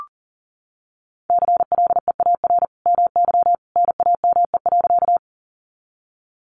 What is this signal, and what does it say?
Morse code, heard as a keyed tone, about 40 wpm, 700 Hz. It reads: CLEAR MY NAME.
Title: Sentences from Top 500 Words 40wpm